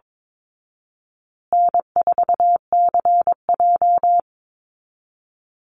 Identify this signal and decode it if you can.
Morse code, heard as a keyed tone, about 22 wpm, 700 Hz. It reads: N4CJ
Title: Easy Call Signs Repeated 2x - 22wpm